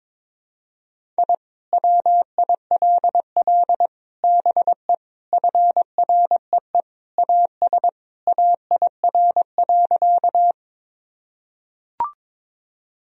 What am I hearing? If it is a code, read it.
Morse code, heard as a keyed tone, about 22 wpm, 700 Hz. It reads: I WILL BE FREE AS AIR.
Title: Sentences from Top 1000 Words - Review Part 1 - 22wpm